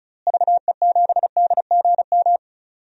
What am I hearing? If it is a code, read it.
Morse code, heard as a keyed tone, about 36 wpm, 700 Hz. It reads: VE7DGM
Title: DXCC-100 Call Signs - 6 Characters - Repeated 2x - 35wpm